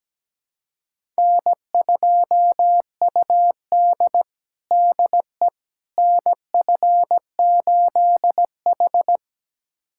Morse code, heard as a keyed tone, about 17 wpm, 700 Hz. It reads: N2UD DE NF8H
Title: USA Callsign DE Callsign - 4 Characters - 17wpm